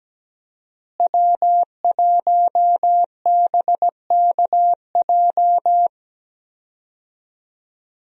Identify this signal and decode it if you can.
Morse code, heard as a keyed tone, about 17 wpm, 700 Hz. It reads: W1BKJ
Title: USA Call Signs - 5 Characters - Repeated 2x - 17wpm